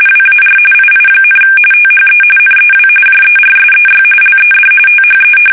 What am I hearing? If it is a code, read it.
RTTY, about 45.45 baud. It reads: WAR II, THE NEW JERSEY LED THE
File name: fsk850 / battleshipnj